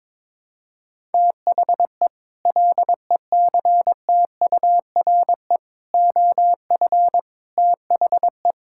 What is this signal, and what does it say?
Morse code, heard as a keyed tone, about 22 wpm, 700 Hz. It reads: THE LECTURE OF THE
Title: Sentences from Top 1600 Words 22wpm